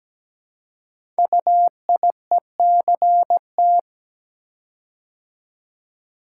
Morse code, heard as a keyed tone, about 17 wpm, 700 Hz. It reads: UIECT
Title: Code Groups - 5 Letters - 17wpm